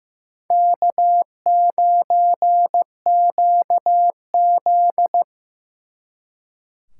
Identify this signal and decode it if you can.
Morse code, heard as a keyed tone, about 18 wpm, 700 Hz. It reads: K9QZ
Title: Speed Racing - USA 1x2 Call Signs 15wpm